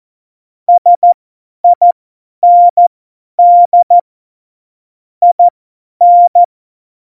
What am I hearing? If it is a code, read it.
Morse code, heard as a keyed tone, about 14 wpm, spaced as for 10 wpm, 700 Hz. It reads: SIND IN
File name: Morsekurs / Witze / Witz 02. Januar 2018, 50 F70